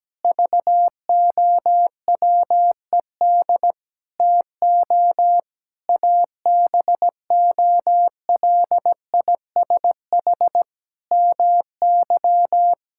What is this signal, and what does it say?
Morse code, heard as a keyed tone, about 17 wpm, 700 Hz. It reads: VOWED TO ABOLISH MY
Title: Pangram Challenge v2 - 17wpm